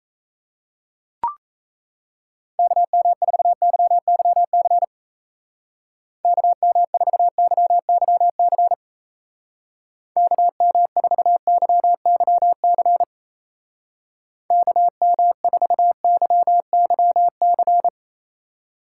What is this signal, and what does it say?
Morse code, heard as a keyed tone, about 35 wpm, 700 Hz. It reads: KM4YYC KM4YYC KM4YYC KM4YYC
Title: Speed Racing - USA Call Signs 28wpm